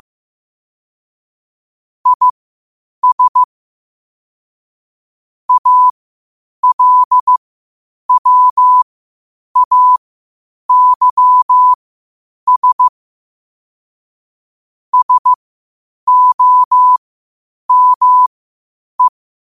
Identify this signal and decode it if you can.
Morse code, heard as a keyed tone, about 15 wpm, spaced as for 8 wpm, 1000 Hz. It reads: IS ALWAYS SOME
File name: qotdfiles / 8wpm